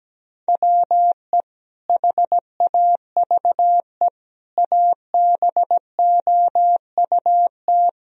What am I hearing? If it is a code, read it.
Morse code, heard as a keyed tone, about 17 wpm, 700 Hz. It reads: WE HAVE ABOUT
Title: Sentences from Top 1000 Words - Review Part 1 - 17wpm